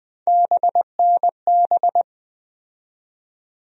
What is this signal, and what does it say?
Morse code, heard as a keyed tone, about 20 wpm, 700 Hz. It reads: BNB
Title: USA Call Sign Postfixes - 3 Letters 20wpm